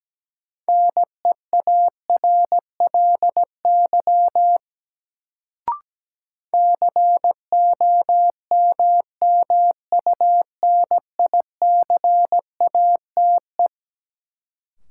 Morse code, heard as a keyed tone, about 17 wpm, 700 Hz. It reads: NEARLY COMMUNICATE
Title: Sentences from Top 1300 Words - New Word Drill - 17wpm